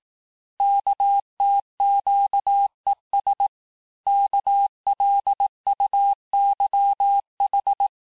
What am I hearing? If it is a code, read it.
Morse code, heard as a keyed tone, about 18 wpm, 800 Hz. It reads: KTQES KLUYH
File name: cours-du-lundi files / cours-1852-texte-chiffre-vitesse-17MPM